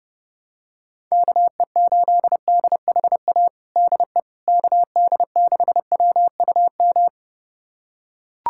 Morse code, heard as a keyed tone, about 30 wpm, 700 Hz. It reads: KE8DHA DE KD6WUM
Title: USA Callsign DE Callsign - 6 Characters - 30wpm